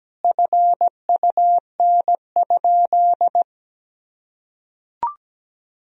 Morse code, heard as a keyed tone, about 17 wpm, 700 Hz. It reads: FUN?
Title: Sentences from Top 1000 Words - Review Part 1 - 17wpm